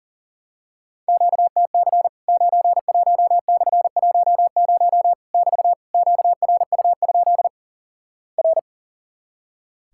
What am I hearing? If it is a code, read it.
Morse code, heard as a keyed tone, about 37 wpm, 700 Hz. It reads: QTC 91/10 <BT> QRU? R
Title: Worked All Europe DX Contest - QTC Exchange - 40wpm